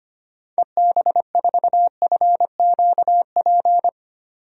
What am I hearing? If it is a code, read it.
Morse code, heard as a keyed tone, about 25 wpm, 700 Hz. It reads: EB4FQP
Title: DXCC-100 Call Signs - 6 Characters - 25wpm